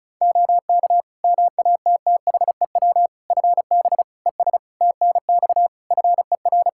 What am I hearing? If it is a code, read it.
Morse code, heard as a keyed tone, about 34 wpm, 700 Hz. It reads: OK MATTHEW FB ES TNX FER
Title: Practice QSO - Exchange 2 - Info - 35wpm